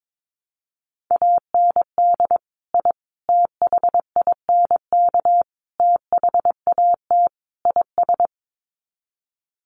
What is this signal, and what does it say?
Morse code, heard as a keyed tone, about 22 wpm, 700 Hz. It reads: AND I THINK THAT IS
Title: Sets of 5 Words Encore2 22wpm